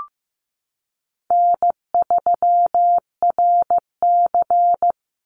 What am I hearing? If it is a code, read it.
Morse code, heard as a keyed tone, about 15 wpm, 700 Hz. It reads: N3RC
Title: Call Signs Repeated 3x - 15wpm